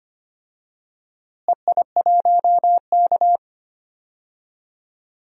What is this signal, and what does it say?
Morse code, heard as a keyed tone, about 25 wpm, 700 Hz. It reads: EI1K
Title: DXCC-100 Call Signs - 4 Characters - Repeated 2x - 25wpm